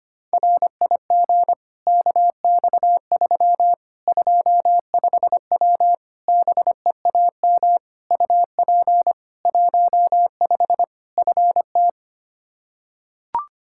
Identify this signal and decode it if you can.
Morse code, heard as a keyed tone, about 25 wpm, 700 Hz. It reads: RIG KX3 25W BEAM UP 15 FT
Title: K1USN SST QSO Practice 25wpm